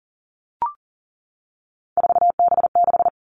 Morse code, heard as a keyed tone, about 41 wpm, 690 Hz. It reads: 4B6
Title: Three Challenging Letter-Number Combinations 40wpm